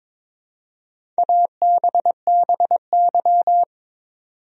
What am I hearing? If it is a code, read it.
Morse code, heard as a keyed tone, about 21 wpm, 700 Hz. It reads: ABBY
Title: USA Common Names - Max 4 Letters - 22wpm